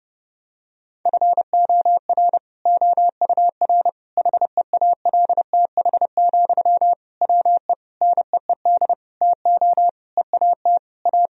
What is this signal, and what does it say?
Morse code, heard as a keyed tone, about 30 wpm, 700 Hz. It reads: FOR OUR HEALTH, WE NEED TO EAT A
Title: Sentences from Top 1000 Words - Review Part 4 - 30wpm